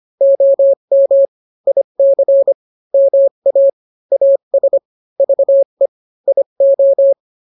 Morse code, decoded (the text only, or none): OM IC MA AS VE IO